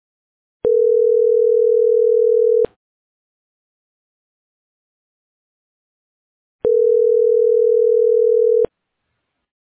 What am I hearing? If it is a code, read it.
Ringback tone (North America), 440 + 480 Hz.